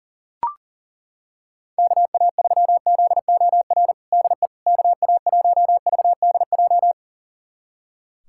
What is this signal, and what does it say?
Morse code, heard as a keyed tone, about 37 wpm, 700 Hz. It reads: KA3ZOR DE KA1VDJ
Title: USA Callsign DE Callsign - 6 Characters - 40wpm